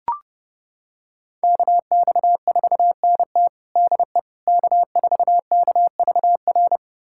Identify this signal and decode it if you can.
Morse code, heard as a keyed tone, about 30 wpm, 700 Hz. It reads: KX4NT DE K4KVR
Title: USA Callsign DE Callsign - 5 Characters - 30wpm